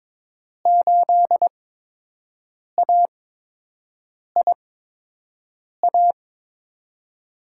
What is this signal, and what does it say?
Morse code, heard as a keyed tone, about 22 wpm, spaced as for 6 wpm, 700 Hz. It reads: 8AIA
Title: USA Call Signs - 8x Character Spacing - 22wpm